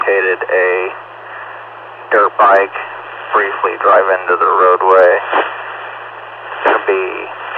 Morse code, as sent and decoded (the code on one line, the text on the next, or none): none